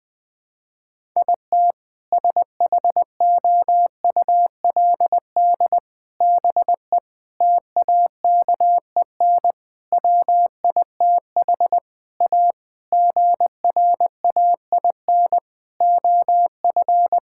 Morse code, heard as a keyed tone, about 20 wpm, 700 Hz. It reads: IT SHOULD BE TAKEN WITH A GRAIN OF
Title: Sentences with 8-Word Phrases - Spoken Beginning - 20wpm